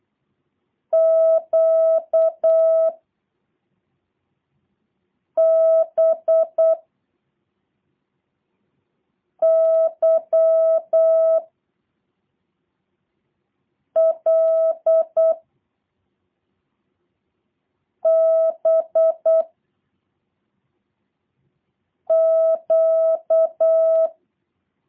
Morse code, heard as a keyed tone, about 8 wpm, 650 Hz. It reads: Q B Y L B Q
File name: jh8cbh / 2023cw / review1-6